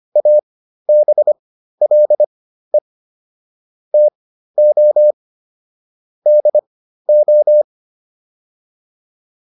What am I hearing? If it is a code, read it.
Morse code, heard as a keyed tone, about 25 wpm, spaced as for 13 wpm, 600 Hz. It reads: ABLE TO DO